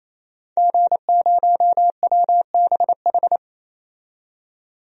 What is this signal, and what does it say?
Morse code, heard as a keyed tone, about 28 wpm, 700 Hz. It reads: G0WBH
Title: DXCC-100 Call Signs - 5 Characters - 28wpm